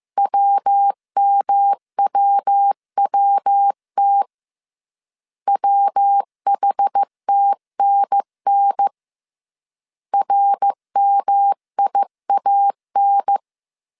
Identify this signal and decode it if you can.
Morse code, heard as a keyed tone, about 15 wpm, 800 Hz. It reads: WMWWT WHTNN RMIAN
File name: CW / DET / 1 3 1500C S A 14wpm 2447